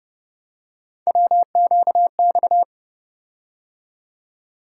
Morse code, heard as a keyed tone, about 29 wpm, 700 Hz. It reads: WQX
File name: Speed-Racing - Three Letter-Number Combinations - 25wpm